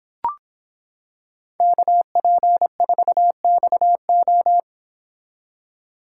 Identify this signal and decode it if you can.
Morse code, heard as a keyed tone, about 26 wpm, 700 Hz. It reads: KP4XO